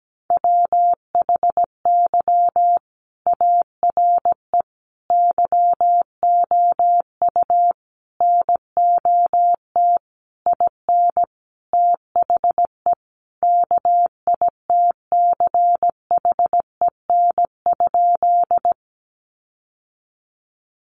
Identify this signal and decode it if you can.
Morse code, heard as a keyed tone, about 17 wpm, 700 Hz. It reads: WHY ARE YOU NOT IN THE KITCHEN?
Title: Sentences from Top 1400 Words 17wpm